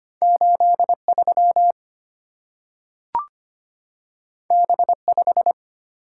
Morse code, heard as a keyed tone, about 25 wpm, 700 Hz. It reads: 8 3 B 5
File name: Two Letter-Number Combinations - 25wpm